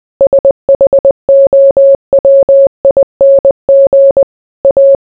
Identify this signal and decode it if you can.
Morse code, heard as a keyed tone, about 20 wpm, 550 Hz. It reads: SHOWING A